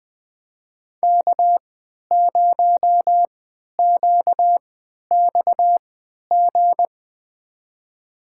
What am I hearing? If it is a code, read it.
Morse code, heard as a keyed tone, about 20 wpm, spaced as for 11 wpm, 700 Hz. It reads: K0QXG
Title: USA Call Signs - 5 Characters - 3x Character Spacing - 20wpm